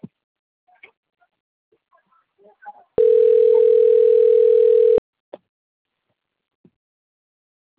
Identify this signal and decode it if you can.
Ringback tone (North America), 440 + 480 Hz.